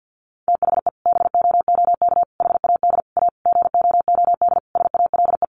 Morse code, heard as a keyed tone, about 50 wpm, 700 Hz. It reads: THE BOOK HAD A GOOD SALE
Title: Sentences from Top 1400 Words 50wpm